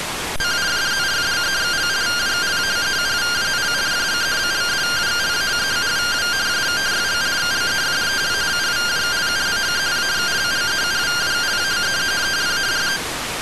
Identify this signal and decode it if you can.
Two-tone teleprinter signal, about 170 Hz shift.